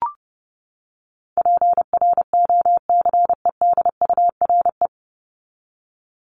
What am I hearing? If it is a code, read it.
Morse code, heard as a keyed tone, about 30 wpm, 700 Hz. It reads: PROCEDURE
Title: Top 1751-2000 Words 30wpm